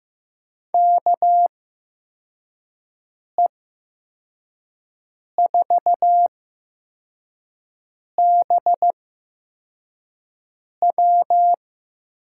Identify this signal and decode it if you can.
Morse code, heard as a keyed tone, about 15 wpm, spaced as for 4 wpm, 700 Hz. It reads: KE4BW
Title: USA Call Signs - 6 Characters - 8x Character Spacing - 15wpm